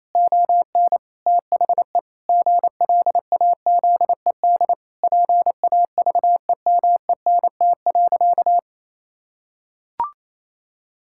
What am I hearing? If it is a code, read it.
Morse code, heard as a keyed tone, about 28 wpm, 700 Hz. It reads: ON THE GLAZED PAVEMENT.
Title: Pangram Challenge v2 - 28wpm